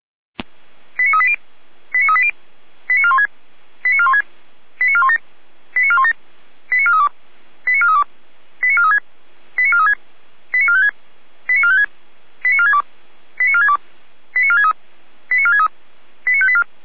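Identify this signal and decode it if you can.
Five-tone selective call (ZVEI), heard as quick runs of five tones, tones of about 70 ms. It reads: 89380 89380 89416 89416 89417 89417 89432 89432 89436 89436 89456 89456 89462 89462 89463 89463 89584
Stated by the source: ZVEI five-tone selective call received by SDR.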